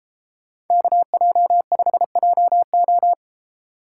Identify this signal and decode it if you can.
Morse code, heard as a keyed tone, about 30 wpm, 700 Hz. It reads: KJ5JO